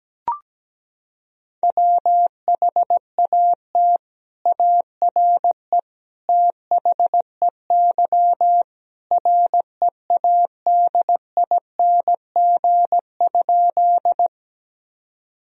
Morse code, heard as a keyed tone, about 17 wpm, 700 Hz. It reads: WHAT ARE THEY READING?
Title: Sentences from Top 1000 Words - Review Part 3 - 17wpm